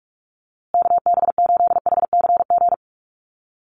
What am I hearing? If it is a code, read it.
Morse code, heard as a keyed tone, about 44 wpm, 690 Hz. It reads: KB8HCG
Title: USA Call Signs Repeated 2x 45wpm